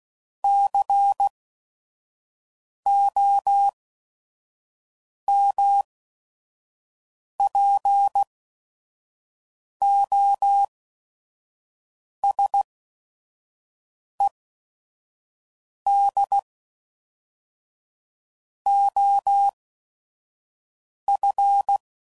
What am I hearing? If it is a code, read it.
Morse code, heard as a keyed tone, about 16 wpm, spaced as for 5 wpm, 800 Hz. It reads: COMPOSED OF
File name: CW Code Groups @5WPM